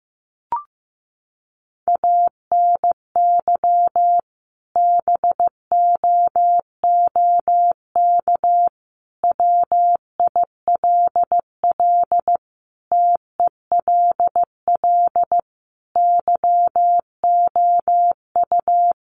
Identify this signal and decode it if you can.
Morse code, heard as a keyed tone, about 15 wpm, 700 Hz. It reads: ANY BOOK WILL TELL YOU
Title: Sentences from Top 300 Words 15wpm